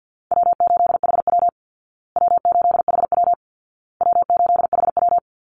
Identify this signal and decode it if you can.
Morse code, heard as a keyed tone, about 50 wpm, 700 Hz. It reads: W8HW W8HW W8HW
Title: Easy Call Signs Repeated 3x - 50wpm